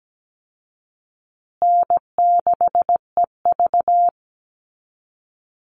Morse code, heard as a keyed tone, about 17 wpm, 700 Hz. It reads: N6EV
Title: Easy Call Signs Repeated 2x - 17wpm